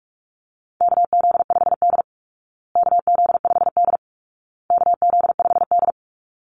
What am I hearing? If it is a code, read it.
Morse code, heard as a keyed tone, about 44 wpm, 700 Hz. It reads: KZ5D KZ5D KZ5D
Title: Call Signs Repeated 2x - 45wpm